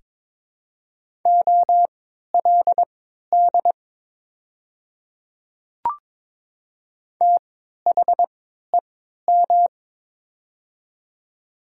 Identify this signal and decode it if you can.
Morse code, heard as a keyed tone, about 22 wpm, spaced as for 13 wpm, 700 Hz. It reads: OLD THEM
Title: 3x Character Spacing - Top 200 Words - 22wpm